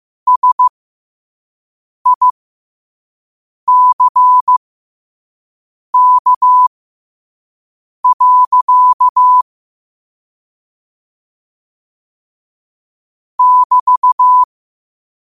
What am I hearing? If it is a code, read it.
Morse code, heard as a keyed tone, about 15 wpm, spaced as for 5 wpm, 1000 Hz. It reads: SICK. <BT>
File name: qotdfiles / 5wpm